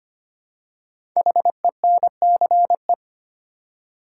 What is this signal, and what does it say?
Morse code, heard as a keyed tone, about 25 wpm, 700 Hz. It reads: HENCE
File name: 5 Letter Words - First 3 Letters Given - 25wpm